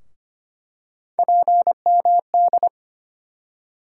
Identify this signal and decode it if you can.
Morse code, heard as a keyed tone, about 25 wpm, 700 Hz. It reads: PMD
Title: USA Call Sign Postfixes - 3 Letters 25wpm